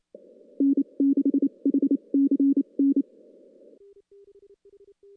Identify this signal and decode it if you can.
Morse code, heard as a keyed tone, about 29 wpm, spaced as for 25 wpm, 280 Hz. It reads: N6HCN
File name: contests / CW-SS-k5tr- -0000